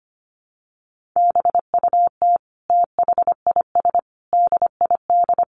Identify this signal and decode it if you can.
Morse code, heard as a keyed tone, about 25 wpm, 700 Hz. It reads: BUT THIS DID
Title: Sentences from Top 1200 Words 25wpm